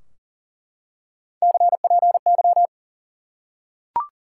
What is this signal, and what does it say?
Morse code, heard as a keyed tone, about 40 wpm, 700 Hz. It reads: CPY
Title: Top 90 QSO Elements 40wpm